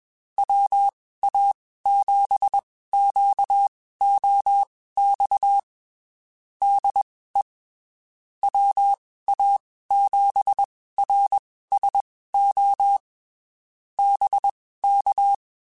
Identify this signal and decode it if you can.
Morse code, heard as a keyed tone, about 21 wpm, spaced as for 15 wpm, 800 Hz. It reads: WA7QOX DE WA7RSO BK
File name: (spaced) / QSOs Spaced-mp3s / WA7QOX 1 @20 sp